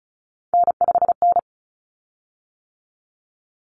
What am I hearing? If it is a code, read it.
Morse code, heard as a keyed tone, about 34 wpm, 700 Hz. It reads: N5N
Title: Common Cut Numbers - 3-Digit 35wpm